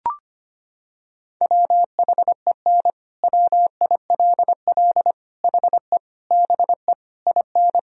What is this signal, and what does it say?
Morse code, heard as a keyed tone, about 25 wpm, 700 Hz. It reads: WHEN WILL HE BE IN
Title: Sentences from Top 1000 Words - Review Part 4 - 25wpm